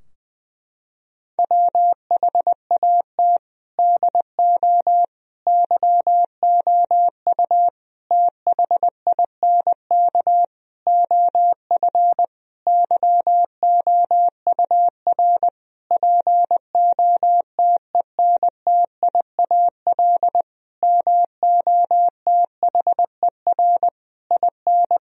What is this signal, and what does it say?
Morse code, heard as a keyed tone, about 21 wpm, 700 Hz. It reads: WHAT DO YOU THINK OF YOUR POTENTIAL MOTHER IN
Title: Sentences from Top 1000 Words - Review Part 4 - 20wpm